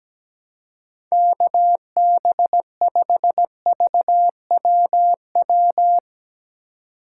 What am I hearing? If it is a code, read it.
Morse code, heard as a keyed tone, about 17 wpm, 700 Hz. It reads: KB5VWW
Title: USA Call Signs - 6 Characters - 4x Character Spacing - 17wpm